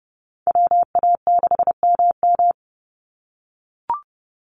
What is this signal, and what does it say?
Morse code, heard as a keyed tone, about 32 wpm, 700 Hz. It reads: WA6MM
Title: NA SOTA Activator Callsigns 30wpm